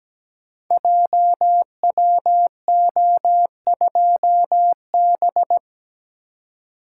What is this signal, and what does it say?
Morse code, heard as a keyed tone, about 17 wpm, 700 Hz. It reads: JWO2B
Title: Code Groups - 5 Letter:Numbers - 17wpm